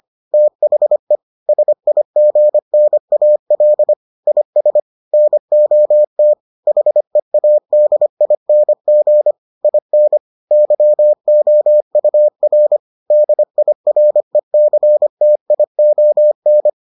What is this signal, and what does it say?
Morse code, heard as a keyed tone, about 25 wpm, 600 Hz. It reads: THE SIGNAL IS NOT HEADING IN YOUR DIRECTION